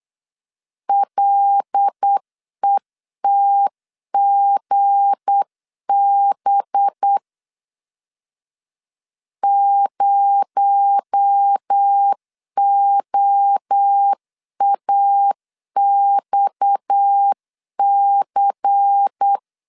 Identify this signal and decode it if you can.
Morse code, heard as a keyed tone, about 8 wpm, 800 Hz. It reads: LETGB 0OAXC
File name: LEZIONI / 7 GF9Z / 7 G 8wpm 3118